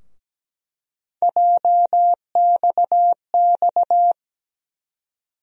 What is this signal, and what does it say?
Morse code, heard as a keyed tone, about 17 wpm, 700 Hz. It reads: JXX
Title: USA Call Sign Postfixes - 3 Letters 17wpm